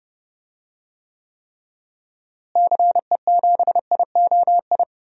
Morse code, heard as a keyed tone, about 31 wpm, 700 Hz. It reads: CE7IOI
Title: DXCC-100 Call Signs - 6 Characters - Repeated 3x - 30wpm